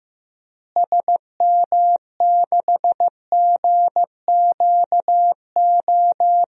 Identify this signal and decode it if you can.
Morse code, heard as a keyed tone, about 15 wpm, 700 Hz. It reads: SM6GQO
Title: DXCC-100 Call Signs - 6 Characters - Repeated 2x - 15wpm